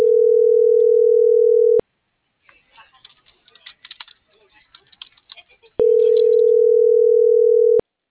Ringback tone (North America), 440 + 480 Hz.